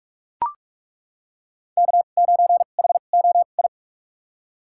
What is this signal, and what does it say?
Morse code, heard as a keyed tone, about 41 wpm, spaced as for 31 wpm, 700 Hz. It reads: K9HOI
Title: USA Call Signs - 5 Characters - 2x Character Spacing - 45wpm